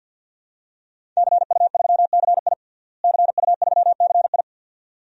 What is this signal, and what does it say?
Morse code, heard as a keyed tone, about 51 wpm, 700 Hz. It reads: CU3CI CU3CI
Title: DXCC-100 Call Signs - 5 Characters - Repeated 2x - 50wpm